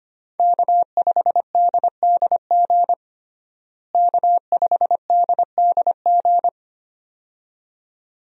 Morse code, heard as a keyed tone, about 25 wpm, 700 Hz. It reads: K5DDG K5DDG
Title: USA Call Signs - 5 Characters - Repeated 2x - 25wpm